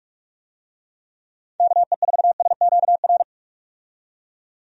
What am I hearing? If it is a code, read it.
Morse code, heard as a keyed tone, about 46 wpm, 700 Hz. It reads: KE4SQR
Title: USA Call Signs - 6 Characters - 45wpm